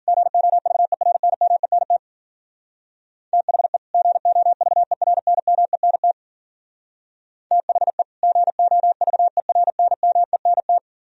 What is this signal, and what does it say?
Morse code, heard as a keyed tone, about 52 wpm, 700 Hz. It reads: GOVERNMENT THE GOVERNMENT THE GOVERNMENT